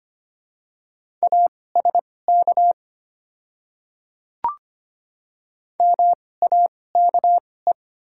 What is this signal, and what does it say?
Morse code, heard as a keyed tone, about 25 wpm, spaced as for 18 wpm, 700 Hz. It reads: ASK MAKE
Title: 2x Character Spacing - Top 500 Words - 25wpm